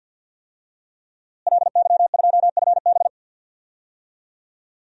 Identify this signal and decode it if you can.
Morse code, heard as a keyed tone, about 47 wpm, 700 Hz. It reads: LY2FB
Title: DXCC-100 Call Signs - 5 Characters - Repeated 2x - 50wpm